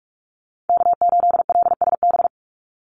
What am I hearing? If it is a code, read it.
Morse code, heard as a keyed tone, about 44 wpm, 700 Hz. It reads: K8LSB
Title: Call Signs Repeated 3x - 45wpm